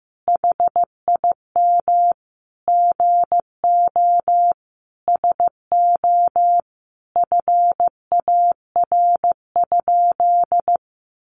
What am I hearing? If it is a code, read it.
Morse code, heard as a keyed tone, about 15 wpm, 700 Hz. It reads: HIM GO SO FAR?
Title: Sentences from Top 300 Words 15wpm